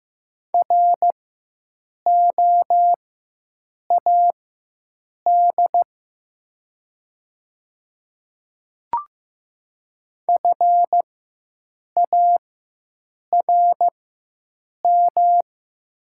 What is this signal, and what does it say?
Morse code, heard as a keyed tone, about 15 wpm, spaced as for 7 wpm, 700 Hz. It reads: ROAD FARM
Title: 4x Character Spacing - Top 500 Words - 15wpm